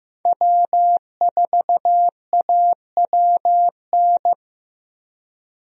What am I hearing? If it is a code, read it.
Morse code, heard as a keyed tone, about 15 wpm, 700 Hz. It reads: W4AWN
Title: USA Call Signs - 5 Characters - Repeated 3x - 15wpm